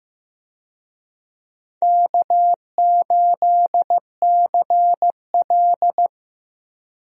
Morse code, heard as a keyed tone, about 15 wpm, 700 Hz. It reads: K8CL
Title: USA Call Signs - 4 Characters - Repeated 2x - 15wpm